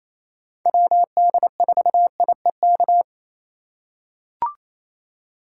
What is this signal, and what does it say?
Morse code, heard as a keyed tone, about 27 wpm, 700 Hz. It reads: WD4IEK
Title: USA Call Signs - 6 Characters - 3x Character Spacing - 28wpm